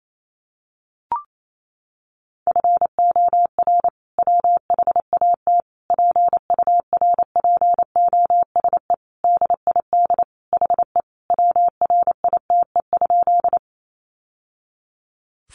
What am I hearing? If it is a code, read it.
Morse code, heard as a keyed tone, about 28 wpm, 700 Hz. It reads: FOR WHAT PURPOSE DID HE WRITE?
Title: Sentences from Top 800 Words 28wpm